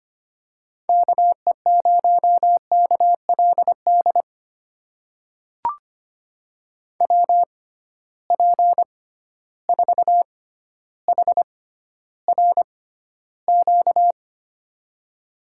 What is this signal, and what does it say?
Morse code, heard as a keyed tone, about 25 wpm, spaced as for 13 wpm, 700 Hz. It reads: KE0KLD WP4HRQ
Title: USA Call Signs - 6 Characters - 6x Character Spacing - 25wpm